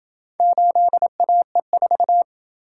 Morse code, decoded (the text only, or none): 8AE4